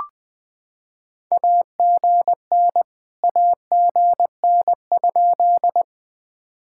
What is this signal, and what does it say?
Morse code, heard as a keyed tone, about 21 wpm, 700 Hz. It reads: AGN AGN?
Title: POTA QSO Practice - 20wpm